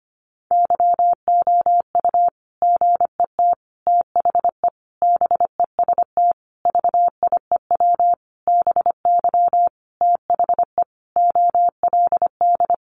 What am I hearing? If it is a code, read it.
Morse code, heard as a keyed tone, about 25 wpm, 700 Hz. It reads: YOU GET THE BEST VIEW BY THE OLD
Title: Sentences from Top 1200 Words 25wpm